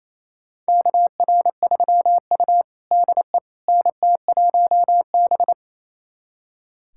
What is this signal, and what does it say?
Morse code, heard as a keyed tone, about 27 wpm, 700 Hz. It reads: KR3U DE NT1B
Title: USA Callsign DE Callsign - 4 Characters - 28wpm